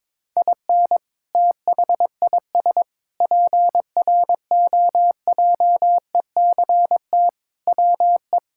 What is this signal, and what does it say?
Morse code, heard as a keyed tone, about 22 wpm, 700 Hz. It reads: IN THIS PROJECT WE
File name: practice / 22wpm / Sentences with 8-Word Phrases - 22wpm